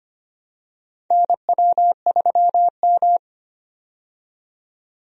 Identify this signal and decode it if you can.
Morse code, heard as a keyed tone, about 25 wpm, 700 Hz. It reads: NW3M
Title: USA Call Signs - 4 Characters - 25wpm